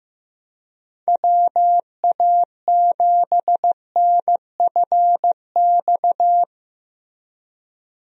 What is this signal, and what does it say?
Morse code, heard as a keyed tone, about 15 wpm, 700 Hz. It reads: WA7NFX
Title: USA Call Signs - 6 Characters - Repeated 2x - 15wpm